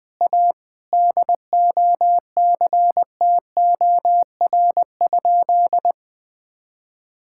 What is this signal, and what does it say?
Morse code, heard as a keyed tone, about 20 wpm, 700 Hz. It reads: A DOCTOR?
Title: Sentences with 4-Word Phrases - 20wpm